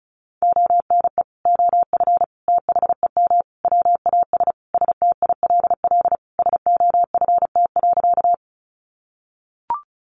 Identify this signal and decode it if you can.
Morse code, heard as a keyed tone, about 34 wpm, 700 Hz. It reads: ONE OF THEM WAS STILL SOFT.
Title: Sentences from Top 1500 Words 35wpm